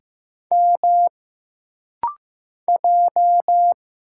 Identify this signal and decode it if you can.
Morse code, heard as a keyed tone, about 15 wpm, 700 Hz. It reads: M J
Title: Single Letters - Rapid-Fire 15wpm